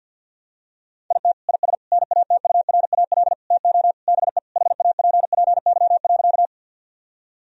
Morse code, heard as a keyed tone, about 50 wpm, 700 Hz. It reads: IT IS NATURAL TO BE HAPPY.
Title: Sentences from Top 1000 Words - Review Part 4 - 50wpm